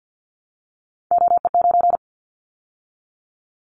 Morse code, heard as a keyed tone, about 50 wpm, 700 Hz. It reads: OE9
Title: Callsign Prefixes for Top 50 DXCC - 50wpm